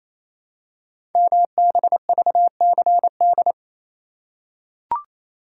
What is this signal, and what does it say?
Morse code, heard as a keyed tone, about 29 wpm, 700 Hz. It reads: MBVCD
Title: Code Groups - 5 Letters - 28wpm